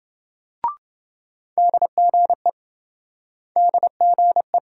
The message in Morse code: DGE DGE

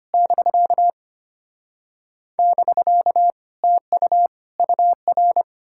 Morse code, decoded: <BK> <BK> TU UR

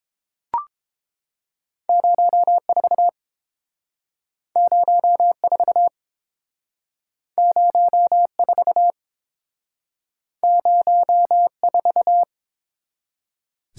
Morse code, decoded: 04 04 04 04

